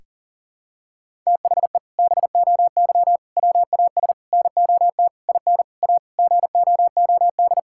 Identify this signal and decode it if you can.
Morse code, heard as a keyed tone, about 37 wpm, 700 Hz. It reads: THE BOY WAS NOT IN A GOOD
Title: Sentences from Top 1000 Words - Review Part 6 - 40wpm